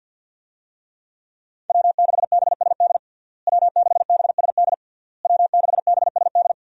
Morse code, heard as a keyed tone, about 53 wpm, 700 Hz. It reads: W6BSD W6BSD W6BSD
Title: USA Call Signs - 5 Characters - Repeated 3x - 50wpm